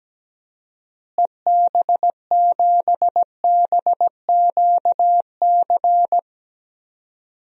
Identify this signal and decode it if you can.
Morse code, heard as a keyed tone, about 17 wpm, 700 Hz. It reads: EB7BQC
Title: DXCC-100 Call Signs - 6 Characters - Repeated 2x - 17wpm